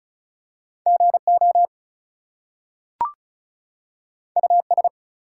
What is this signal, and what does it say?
Morse code, heard as a keyed tone, about 36 wpm, 700 Hz. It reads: GO US
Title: Two Letter Words - 35wpm